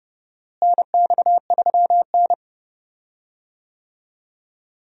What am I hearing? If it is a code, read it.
Morse code, heard as a keyed tone, about 31 wpm, 700 Hz. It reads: NX3N
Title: USA Call Signs - 4 Characters - Repeated 2x - 30wpm